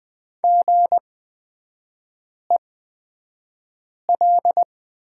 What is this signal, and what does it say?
Morse code, heard as a keyed tone, about 20 wpm, spaced as for 5 wpm, 700 Hz. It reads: GEL
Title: 255 Lesson 41e - Period - Three Characters 20wpm